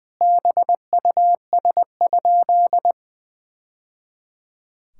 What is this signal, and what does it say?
Morse code, heard as a keyed tone, about 20 wpm, 700 Hz. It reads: BUS?
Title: Sentences from Top 1000 Words - Review Part 6 - 20wpm